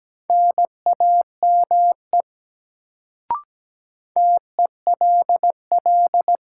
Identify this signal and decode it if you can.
Morse code, heard as a keyed tone, about 17 wpm, 700 Hz. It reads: NAME TELL
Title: Sentences from Top 200 Words - New Word Drill - 17wpm